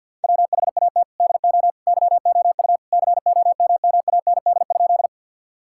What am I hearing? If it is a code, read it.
Morse code, heard as a keyed tone, about 50 wpm, 700 Hz. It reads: WHAT DO YOU COMMAND?
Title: Sentences from Top 1600 Words 50wpm